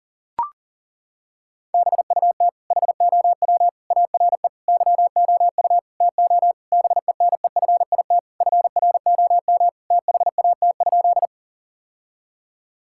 Morse code, heard as a keyed tone, about 37 wpm, 700 Hz. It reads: BUT HOW ARE YOU TO BENEFIT FROM THAT?
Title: Sentences from Top 1000 Words - Review Part 6 - 40wpm